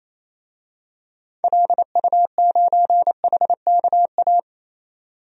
Morse code, heard as a keyed tone, about 28 wpm, 700 Hz. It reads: LU9HKA
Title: DXCC-100 Call Signs - 6 Characters - 28wpm